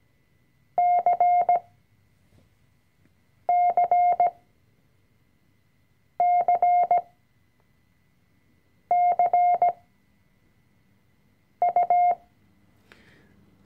Morse code, heard as a keyed tone, about 17 wpm, spaced as for 4 wpm, 710 Hz. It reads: CCCCU
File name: CW-Online-Kurs / Anfänger / Lektion 004